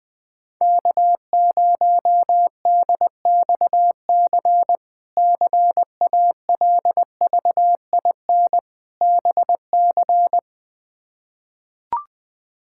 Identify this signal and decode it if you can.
Morse code, heard as a keyed tone, about 20 wpm, 700 Hz. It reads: K0DXC CALVIN BC
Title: K1USN SST QSO Practice - 20wpm